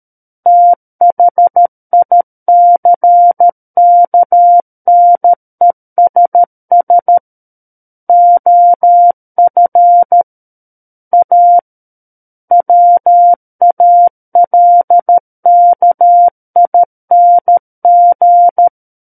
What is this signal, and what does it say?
Morse code, heard as a keyed tone, about 13 wpm, 700 Hz. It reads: THICKNESS OF A WALKING
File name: Book One / 04-Chapter 4 - 13 WPM